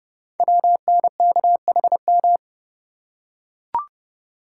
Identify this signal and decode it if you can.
Morse code, heard as a keyed tone, about 30 wpm, 700 Hz. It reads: WNKHM
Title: Code Groups - 5 Letters - 30wpm